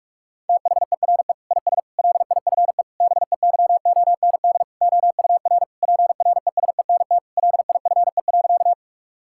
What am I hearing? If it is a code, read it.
Morse code, heard as a keyed tone, about 46 wpm, 690 Hz. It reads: THERE IS LIFE BEYOND OUR PRESENT LIFE.
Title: Sentences from Top 1000 Words - Review Part 6 - 45wpm